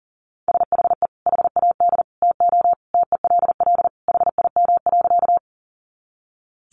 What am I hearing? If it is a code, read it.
Morse code, heard as a keyed tone, about 40 wpm, 700 Hz. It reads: SHE HAD TO TELL HIM.
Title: Sentences from Top 200 Words 40wpm